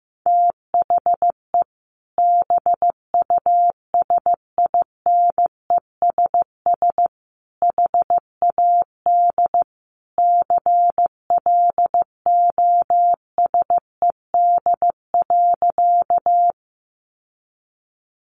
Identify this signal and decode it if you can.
Morse code, heard as a keyed tone, about 15 wpm, 700 Hz. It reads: THE BUSINESS HAD CLOSED.
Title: Sentences from Top 1400 Words 15wpm